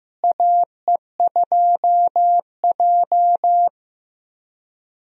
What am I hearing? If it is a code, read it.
Morse code, heard as a keyed tone, about 15 wpm, 700 Hz. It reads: AE2J
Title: USA Call Signs - 4 Characters - Repeated 3x - 15wpm